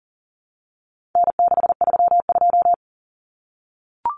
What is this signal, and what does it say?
Morse code, heard as a keyed tone, about 40 wpm, 700 Hz. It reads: N632
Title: Common Cut Numbers - 4-Digit 40wpm